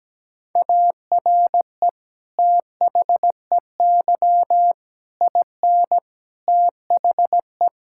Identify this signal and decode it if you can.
Morse code, heard as a keyed tone, about 17 wpm, 700 Hz. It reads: ARE THEY IN THE
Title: Sentences from Top 1600 Words 17wpm